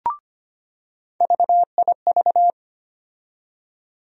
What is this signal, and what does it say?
Morse code, heard as a keyed tone, about 25 wpm, 700 Hz. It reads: VIV
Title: USA Call Sign Postfixes - 3 Letters 25wpm